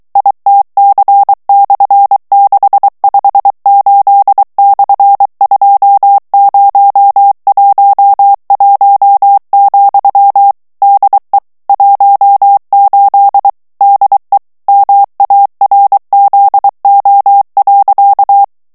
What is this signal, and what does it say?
Morse code, heard as a keyed tone, about 23 wpm, 800 Hz. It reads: ITC/658/2011, DE 18 DE MARZO.
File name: morse mp3 / ESPAGNOL TEXTE EN CLAIR 2 VITESSE 22 MOTS MINUTES